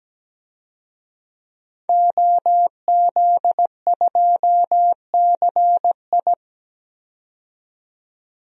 Morse code, heard as a keyed tone, about 17 wpm, 700 Hz. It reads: OZ2CI
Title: DXCC-100 Call Signs - 5 Characters - 17wpm